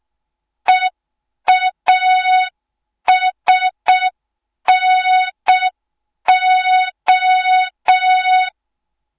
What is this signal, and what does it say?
Morse code, heard as a keyed tone, about 6 wpm, 3700 Hz. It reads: EASNO
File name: Morze-HA2MN / hang / 06BDU HA2MN